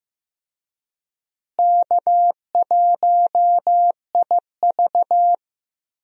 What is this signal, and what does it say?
Morse code, heard as a keyed tone, about 16 wpm, 700 Hz. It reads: K1IV
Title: USA Call Signs - 4 Characters - Repeated 3x - 15wpm